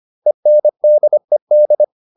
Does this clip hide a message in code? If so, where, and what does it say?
Yes: Morse code from 0.3 to 1.8 s, heard as a keyed tone, about 25 wpm, 600 Hz. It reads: ENDED